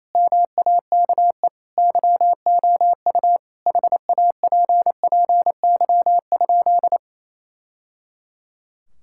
Morse code, heard as a keyed tone, about 28 wpm, 700 Hz. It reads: MAKE YOU HAPPY?
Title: Sentences from Top 1000 Words - Review Part 5 - 28wpm